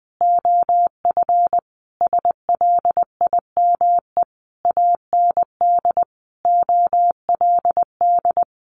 Morse code, heard as a keyed tone, about 20 wpm, 700 Hz. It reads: OF SLIME AND OLD
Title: A Sound Of Thunder 20wpm